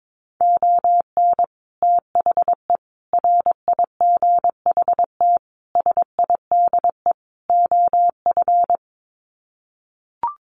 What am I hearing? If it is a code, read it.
Morse code, heard as a keyed tone, about 22 wpm, 700 Hz. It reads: ON THE RIGHT SIDE OF
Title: Sets of 5 Words Encore 22wpm